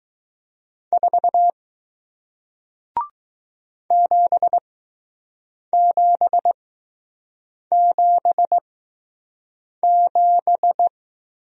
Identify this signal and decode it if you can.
Morse code, heard as a keyed tone, about 20 wpm, spaced as for 6 wpm, 700 Hz. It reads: 4 7777